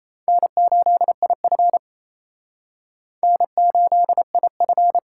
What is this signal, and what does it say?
Morse code, heard as a keyed tone, about 34 wpm, 700 Hz. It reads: N8IF N8IF
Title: Speed Racing - USA 1x2 Call Signs 28wpm